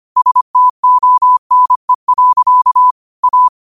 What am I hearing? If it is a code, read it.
Morse code, heard as a keyed tone, about 25 wpm, 1000 Hz. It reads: STONE. A
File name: qotdfiles / 25wpm